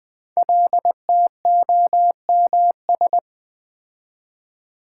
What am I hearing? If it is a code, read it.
Morse code, heard as a keyed tone, about 20 wpm, 700 Hz. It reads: LTOMS
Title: Code Groups - 5 Letter:Numbers - 20wpm